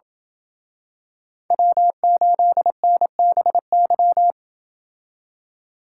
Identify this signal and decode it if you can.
Morse code, heard as a keyed tone, about 27 wpm, 700 Hz. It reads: W8NBY